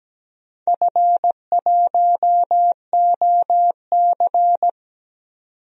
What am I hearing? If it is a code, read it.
Morse code, heard as a keyed tone, about 17 wpm, 700 Hz. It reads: F1OC
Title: DXCC-100 Call Signs - 4 Characters - 17wpm